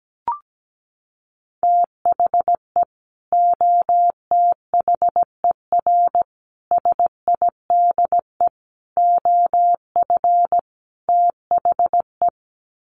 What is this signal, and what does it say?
Morse code, heard as a keyed tone, about 17 wpm, 700 Hz. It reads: THE OTHER SIDE OF THE
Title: Sets of 5 Words 17wpm